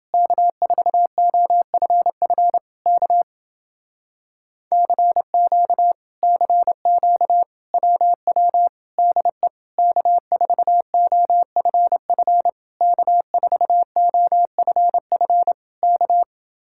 Morse code, heard as a keyed tone, about 29 wpm, 700 Hz. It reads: K4OFF K CQ CQ WW DE K4OFF K4OFF K